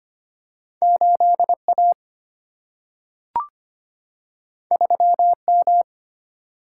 Morse code, heard as a keyed tone, about 25 wpm, 700 Hz. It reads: 8A 3M
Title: Single-Letter Single-Number 25wpm